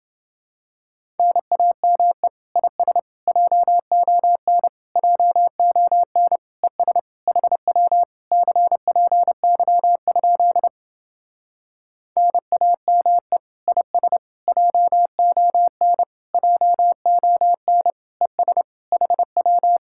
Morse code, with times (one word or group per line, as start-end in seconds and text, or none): 1.2-2.3 NAME
2.6-3.0 IS
3.3-4.7 JON
5.0-6.4 JON
6.6-7.0 ES
7.3-8.0 HW
8.3-10.7 CPY?
12.2-13.4 NAME
13.7-14.2 IS
14.5-16.0 JON
16.3-17.9 JON
18.2-18.6 ES
18.9-19.8 HW